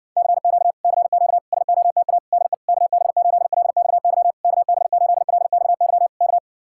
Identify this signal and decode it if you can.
Morse code, heard as a keyed tone, about 60 wpm, 700 Hz. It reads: CQ CQ SOTA DE KB8LXY KB8LXY K